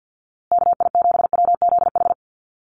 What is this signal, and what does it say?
Morse code, heard as a keyed tone, about 50 wpm, 700 Hz. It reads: KI7WZH
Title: USA Call Signs Repeated 2x 50wpm